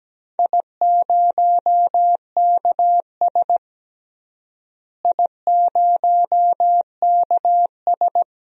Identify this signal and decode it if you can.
Morse code, heard as a keyed tone, about 17 wpm, 700 Hz. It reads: I0KS I0KS
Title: DXCC-100 Call Signs - 4 Characters - Repeated 2x - 17wpm